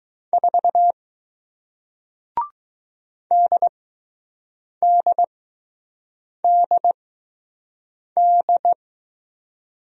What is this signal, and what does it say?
Morse code, heard as a keyed tone, about 20 wpm, spaced as for 6 wpm, 700 Hz. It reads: 4 DDDD